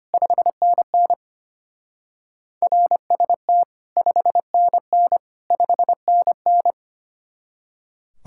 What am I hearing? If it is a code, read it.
Morse code, heard as a keyed tone, about 32 wpm, 700 Hz. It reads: 5NN RST 5NN 5NN